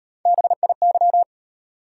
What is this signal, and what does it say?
Morse code, heard as a keyed tone, about 33 wpm, 700 Hz. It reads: BIY